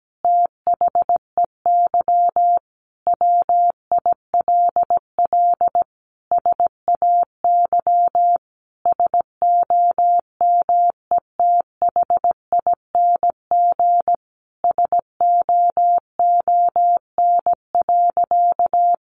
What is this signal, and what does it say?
Morse code, heard as a keyed tone, about 17 wpm, 700 Hz. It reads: THEY WILL SAY SOMETHING SOON.
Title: Sentences from Top 1500 Words 17wpm